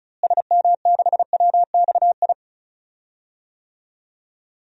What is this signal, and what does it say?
Morse code, heard as a keyed tone, about 36 wpm, 700 Hz. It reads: SM6WXI
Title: DXCC-100 Call Signs - 6 Characters - Repeated 2x - 35wpm